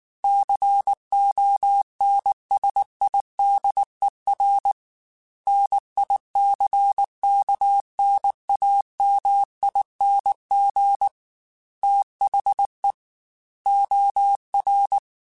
Morse code, heard as a keyed tone, about 19 wpm, 800 Hz. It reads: CONSIDER NICKNAMING THE OR
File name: Technical Subjects / Technical Subjects-mp3s / OR Gate @18